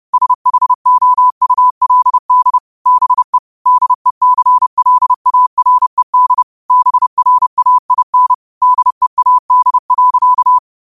Morse code, heard as a keyed tone, about 30 wpm, 1000 Hz. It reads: SHOULD BE DECLARED BRAIN DEAD.